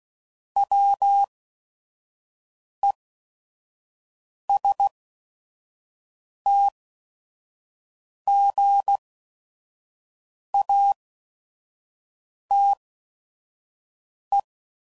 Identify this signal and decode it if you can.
Morse code, heard as a keyed tone, about 16 wpm, spaced as for 5 wpm, 800 Hz. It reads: WESTGATE
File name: CW Code Groups @5WPM / fromSM's / Cities